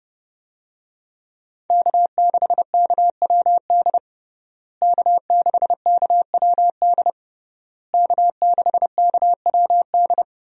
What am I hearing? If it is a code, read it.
Morse code, heard as a keyed tone, about 30 wpm, 700 Hz. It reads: K6KWD K6KWD K6KWD
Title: USA Call Signs - 5 Characters - Repeated 3x - 30wpm